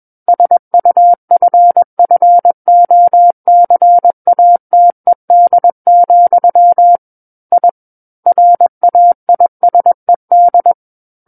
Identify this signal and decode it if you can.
Morse code, heard as a keyed tone, about 21 wpm, 700 Hz. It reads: SUFFOCATED, I RAISED
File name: Book One / 12-Chapter 12 - 21 WPM